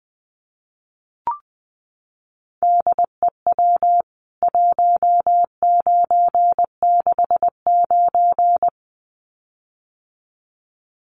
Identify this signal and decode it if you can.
Morse code, heard as a keyed tone, about 20 wpm, 700 Hz. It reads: DEW 1969
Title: CWT Exchange 20wpm